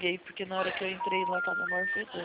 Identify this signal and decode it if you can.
Special information tone, with speech.
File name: out- -2015- - - .93816